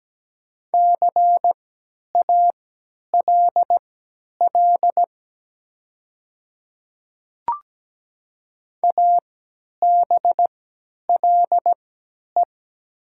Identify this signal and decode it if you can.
Morse code, heard as a keyed tone, about 17 wpm, spaced as for 10 wpm, 700 Hz. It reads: CALL ABLE
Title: 3x Character Spacing - Top 500 Words - 17wpm